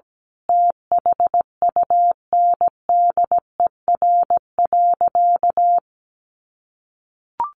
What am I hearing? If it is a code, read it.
Morse code, heard as a keyed tone, about 17 wpm, 700 Hz. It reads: THUNDER.
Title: A Sound Of Thunder 17wpm